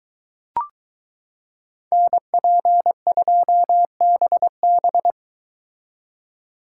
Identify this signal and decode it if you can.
Morse code, heard as a keyed tone, about 23 wpm, 700 Hz. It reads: NP2BB